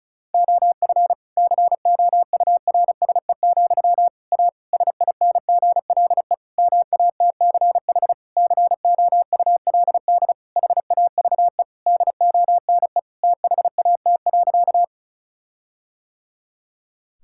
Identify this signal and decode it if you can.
Morse code, heard as a keyed tone, about 36 wpm, 700 Hz. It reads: OF COURSE, A SINGLE MATCH COULD HAVE DONE THAT.
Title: Sentences from Top 1600 Words 35wpm